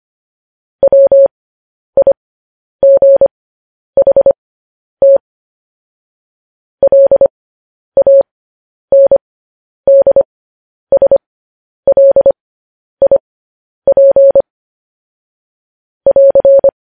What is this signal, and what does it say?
Morse code, heard as a keyed tone, about 25 wpm, spaced as for 10 wpm, 550 Hz. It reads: WIGHT LANDSLIP <AR>